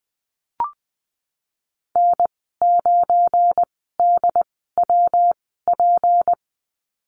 Morse code, heard as a keyed tone, about 20 wpm, spaced as for 15 wpm, 700 Hz. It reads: N9DWP
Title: 173 Lesson 28g - Number 9 - Call Signs 20wpm